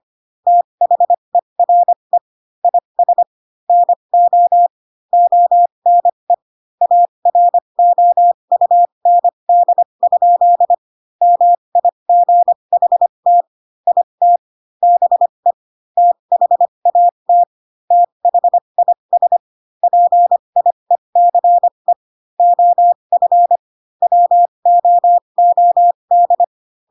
Morse code, heard as a keyed tone, about 25 wpm, spaced as for 22 wpm, 700 Hz. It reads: THERE IS NO ONE AROUND? MIGHT IT BE THAT THIS PIECE OF WOOD